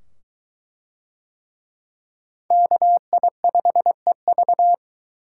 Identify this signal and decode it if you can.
Morse code, heard as a keyed tone, about 23 wpm, 700 Hz. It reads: KI5EV